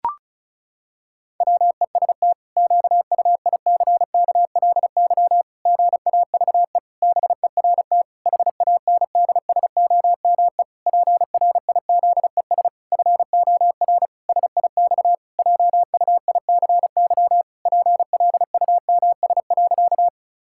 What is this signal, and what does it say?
Morse code, heard as a keyed tone, about 34 wpm, 700 Hz. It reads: WEST QUICKLY GAVE BERT HANDSOME PRIZES FOR SIX JUICY PLUMS.
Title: Pangram Challenge v2 - 35wpm